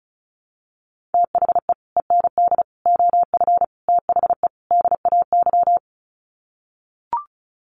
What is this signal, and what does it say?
Morse code, heard as a keyed tone, about 34 wpm, 700 Hz. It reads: THE END OF THE DAY
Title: Sets of 5 Words 35wpm